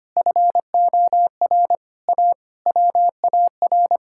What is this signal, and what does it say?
Morse code, heard as a keyed tone, about 25 wpm, 700 Hz. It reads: FOR A WAR
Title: Sentences from Top 1000 Words - Review Part 6 - 25wpm